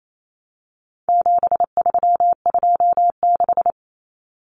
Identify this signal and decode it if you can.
Morse code, heard as a keyed tone, about 28 wpm, 700 Hz. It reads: 7326
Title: Common Cut Numbers - 4-Digit 28wpm